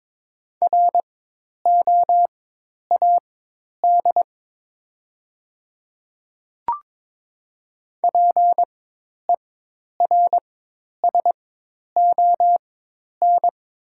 Morse code, heard as a keyed tone, about 22 wpm, spaced as for 10 wpm, 700 Hz. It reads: ROAD PERSON
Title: 4x Character Spacing - Top 500 Words - 22wpm